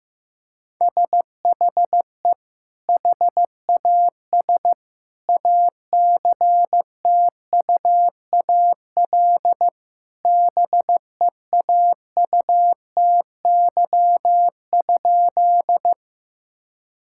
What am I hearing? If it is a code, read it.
Morse code, heard as a keyed tone, about 16 wpm, 700 Hz. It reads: SHE HAS ACTUAL BEAUTY?
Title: Sentences from Top 1000 Words - Review Part 5 - 15wpm